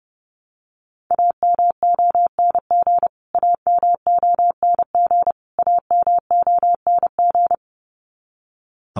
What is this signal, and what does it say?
Morse code, heard as a keyed tone, about 30 wpm, 700 Hz. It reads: AMONG AMONG AMONG
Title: Top 500 Words Repeated 30wpm